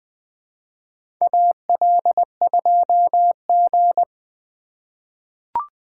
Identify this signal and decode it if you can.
Morse code, heard as a keyed tone, about 20 wpm, 700 Hz. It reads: AL2G
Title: USA Call Signs - 4 Characters - 2x Character Spacing - 20wpm